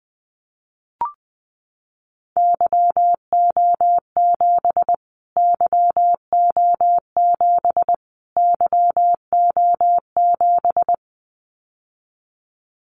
Morse code, heard as a keyed tone, about 20 wpm, 700 Hz. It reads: YO7 YO7 YO7
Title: Callsign Prefixes for Top 25 DXCC - Repeated 3x - 20wpm